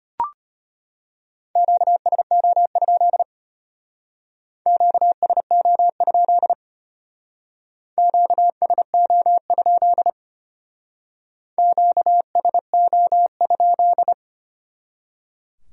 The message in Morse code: QSO? QSO? QSO? QSO?

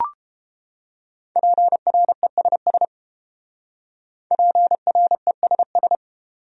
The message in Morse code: PRESS PRESS

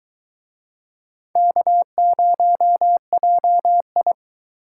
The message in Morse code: K0JI

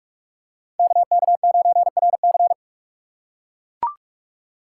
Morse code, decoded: KK9RC

